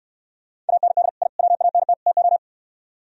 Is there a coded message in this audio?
Yes: Morse code from 0.7 to 2.4 s, heard as a keyed tone, about 66 wpm, 700 Hz. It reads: AND I WANT TO